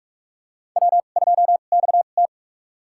Morse code, heard as a keyed tone, about 46 wpm, spaced as for 33 wpm, 700 Hz. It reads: W2XT
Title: USA Call Signs - 4 Characters - 2x Character Spacing - 45wpm